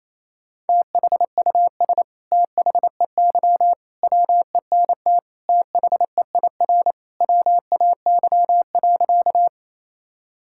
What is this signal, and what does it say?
Morse code, heard as a keyed tone, about 28 wpm, 700 Hz. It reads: THUS THEY WENT THEIR WAY.
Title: Sentences from Top 1000 Words - Review Part 5 - 28wpm